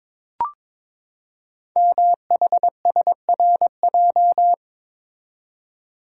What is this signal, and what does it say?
Morse code, heard as a keyed tone, about 22 wpm, 700 Hz. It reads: MHSRJ
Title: Code Groups - 5 Letters - 22wpm